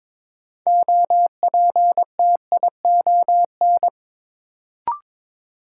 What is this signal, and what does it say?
Morse code, heard as a keyed tone, about 22 wpm, 700 Hz. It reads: OPTION
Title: Sentences from Top 1300 Words - New Word Drill - 22wpm